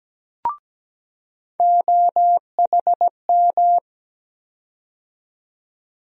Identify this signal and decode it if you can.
Morse code, heard as a keyed tone, about 17 wpm, 700 Hz. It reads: OHM
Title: USA Call Sign Postfixes - 3 Letters 17wpm